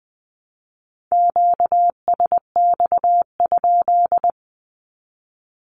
Morse code, heard as a keyed tone, about 20 wpm, 700 Hz. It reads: QSX?
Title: Q-Codes 20wpm